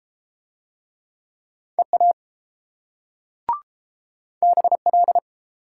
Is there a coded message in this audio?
Yes: Morse code from 1.8 to 5.2 s, heard as a keyed tone, about 31 wpm, 690 Hz. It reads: EA BL